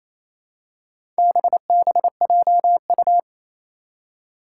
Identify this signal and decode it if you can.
Morse code, heard as a keyed tone, about 28 wpm, 700 Hz. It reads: BBJU
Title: Four Challenging Letter-Number Combinations 28wpm